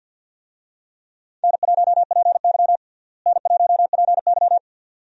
Morse code, heard as a keyed tone, about 51 wpm, 700 Hz. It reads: N1PY N1PY
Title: USA Call Signs - 4 Characters - Repeated 2x - 50wpm